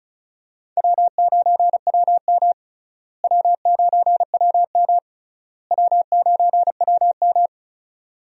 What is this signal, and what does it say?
Morse code, heard as a keyed tone, about 34 wpm, 700 Hz. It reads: W9WM W9WM W9WM
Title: USA Call Signs - 4 Characters - Repeated 3x - 35wpm